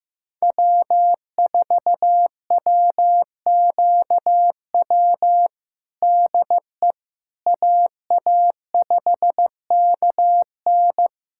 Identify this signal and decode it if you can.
Morse code, heard as a keyed tone, about 15 wpm, 700 Hz. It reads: W4WQW DE AA5KN
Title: USA Callsign DE Callsign - 5 Characters - 15wpm